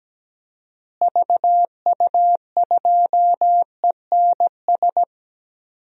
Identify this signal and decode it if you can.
Morse code, heard as a keyed tone, about 17 wpm, 700 Hz. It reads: VU2ENS
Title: DXCC-100 Call Signs - 6 Characters - Repeated 3x - 17wpm